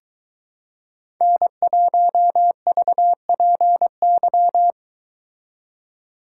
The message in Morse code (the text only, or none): N1VPY